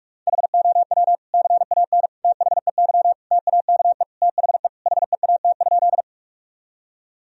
Morse code, heard as a keyed tone, about 42 wpm, 700 Hz. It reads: HOW CAN THEY TAKE THE HEAT?
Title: Sentences from Top 1000 Words - Review Part 3 - 45wpm